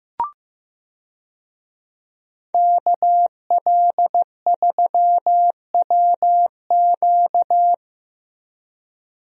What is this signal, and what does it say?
Morse code, heard as a keyed tone, about 15 wpm, 700 Hz. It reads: KL3WQ
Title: USA Call Signs - 5 Characters - Repeated 3x - 15wpm